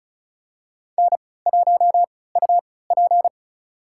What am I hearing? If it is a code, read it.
Morse code, heard as a keyed tone, about 36 wpm, spaced as for 20 wpm, 700 Hz. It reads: N1UP
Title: USA Call Signs - 3x Character Spacing - 35wpm